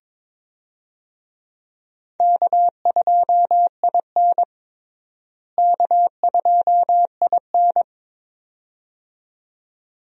Morse code, heard as a keyed tone, about 22 wpm, 700 Hz. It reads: K2IN K2IN
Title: USA Call Signs - 4 Characters - Repeated 2x - 22wpm